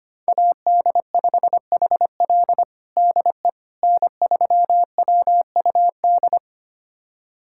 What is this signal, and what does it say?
Morse code, heard as a keyed tone, about 25 wpm, 700 Hz. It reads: AD5HL DE N3WUD
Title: USA Callsign DE Callsign - 5 Characters - 25wpm